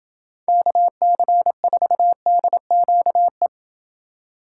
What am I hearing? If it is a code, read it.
Morse code, heard as a keyed tone, about 28 wpm, 700 Hz. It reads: KC4DQE